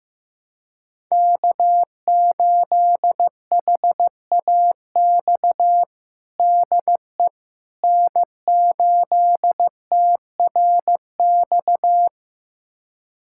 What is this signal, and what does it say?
Morse code, heard as a keyed tone, about 16 wpm, 700 Hz. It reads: K8HAX DE N8TRX
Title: USA Callsign DE Callsign - 5 Characters - 15wpm